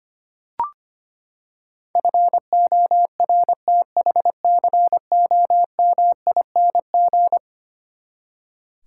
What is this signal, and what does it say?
Morse code, heard as a keyed tone, about 25 wpm, 700 Hz. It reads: FORTHCOMING
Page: morse-code-ninja-practice-25wpm